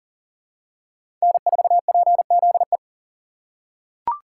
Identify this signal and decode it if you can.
Morse code, heard as a keyed tone, about 40 wpm, 700 Hz. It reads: N4PZE
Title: USA Call Signs - 5 Characters - 4x Character Spacing - 40wpm